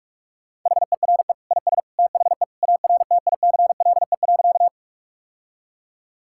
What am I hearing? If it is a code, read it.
Morse code, heard as a keyed tone, about 41 wpm, 700 Hz. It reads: HERE IS THE ARTICLE.
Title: Sentences from Top 1000 Words - Review Part 5 - 45wpm